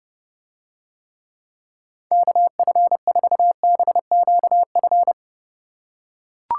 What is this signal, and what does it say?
Morse code, heard as a keyed tone, about 30 wpm, 700 Hz. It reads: KF4BQF